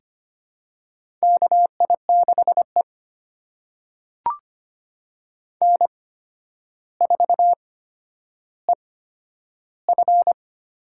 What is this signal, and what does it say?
Morse code, heard as a keyed tone, about 25 wpm, spaced as for 11 wpm, 700 Hz. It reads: KI6E N4EF
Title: USA Call Signs - 4 Characters - 8x Character Spacing - 25wpm